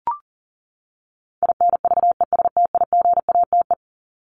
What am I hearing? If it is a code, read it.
Morse code, heard as a keyed tone, about 37 wpm, 700 Hz. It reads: INVESTIGATE
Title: Top 1751-2000 Words 40wpm